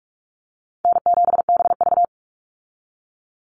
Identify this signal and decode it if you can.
Morse code, heard as a keyed tone, about 44 wpm, 700 Hz. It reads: N7BV
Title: Easy Call Signs - 45wpm